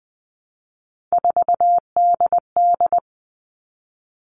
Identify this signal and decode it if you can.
Morse code, heard as a keyed tone, about 20 wpm, 700 Hz. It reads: 4DD
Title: All Cut Numbers - 3-Digit 20wpm